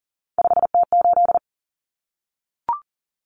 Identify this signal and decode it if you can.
Morse code, heard as a keyed tone, about 40 wpm, 700 Hz. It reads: 5T8
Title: Common Cut Numbers - 3-Digit 40wpm